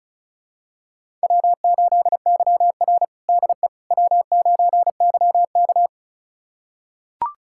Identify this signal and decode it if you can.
Morse code, heard as a keyed tone, about 36 wpm, 700 Hz. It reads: W8YR DE W9YK
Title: USA Callsign DE Callsign - 4 Characters - 35wpm